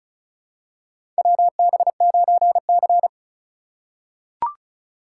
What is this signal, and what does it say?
Morse code, heard as a keyed tone, about 36 wpm, 700 Hz. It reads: WB9C
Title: USA Call Signs - 4 Characters - 2x Character Spacing - 35wpm